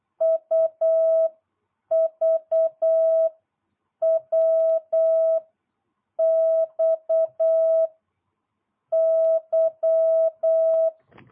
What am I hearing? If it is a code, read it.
Morse code, heard as a keyed tone, about 8 wpm, 650 Hz. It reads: U V W X Y